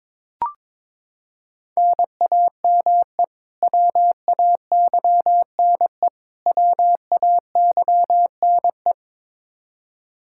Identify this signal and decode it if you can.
Morse code, heard as a keyed tone, about 22 wpm, 700 Hz. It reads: NAME WAYNE WAYNE
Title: Practice QSO - Exchange 1 - Beginning - 22wpm